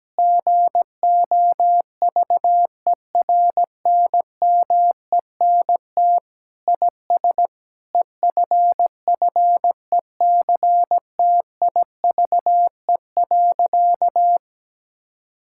Morse code, heard as a keyed tone, about 17 wpm, 700 Hz. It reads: GOVERNMENT IS EFFECTIVE.
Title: Sentences from Top 1000 Words - Review Part 6 - 17wpm